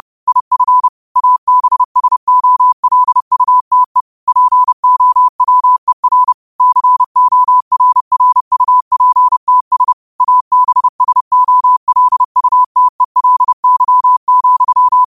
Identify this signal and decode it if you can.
Morse code, heard as a keyed tone, about 30 wpm, 1000 Hz. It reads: IF ABSOLUTE POWER CORRUPTS ABSOLUTELY,